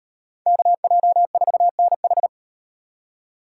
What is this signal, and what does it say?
Morse code, heard as a keyed tone, about 38 wpm, 700 Hz. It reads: KJ4NH